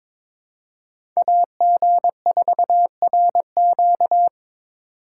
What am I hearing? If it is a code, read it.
Morse code, heard as a keyed tone, about 26 wpm, 700 Hz. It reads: AG4RQ
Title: Speed Racing - USA 2x2 Call Signs 22wpm